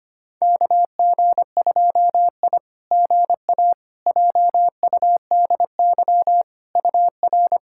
Morse code, heard as a keyed tone, about 25 wpm, 690 Hz. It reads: KG2I GA JUDY UR
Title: SOTA QSO Practice 25wpm